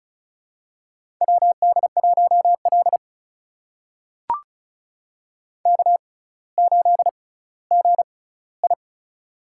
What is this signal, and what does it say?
Morse code, heard as a keyed tone, about 36 wpm, spaced as for 18 wpm, 700 Hz. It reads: WD1L K8GI
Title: USA Call Signs - 4 Characters - 6x Character Spacing - 35wpm